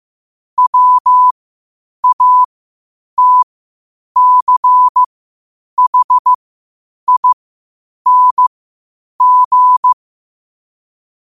Morse code, heard as a keyed tone, about 15 wpm, spaced as for 8 wpm, 1000 Hz. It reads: WATCHING
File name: qotdfiles / 8wpm